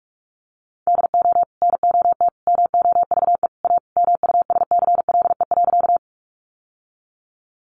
Morse code, heard as a keyed tone, about 44 wpm, 700 Hz. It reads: DO NOT MOVE A MUSCLE.
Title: Sentences from Top 1300 Words 45wpm